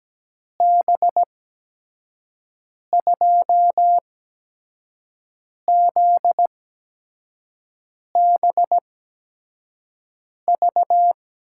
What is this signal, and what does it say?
Morse code, heard as a keyed tone, about 17 wpm, spaced as for 5 wpm, 700 Hz. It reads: B2ZBV
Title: USA Call Signs - 8x Character Spacing - 17wpm